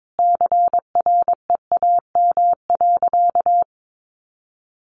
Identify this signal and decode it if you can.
Morse code, heard as a keyed tone, about 22 wpm, 700 Hz. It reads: CREAM.
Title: Sentences from Top 1500 Words 22wpm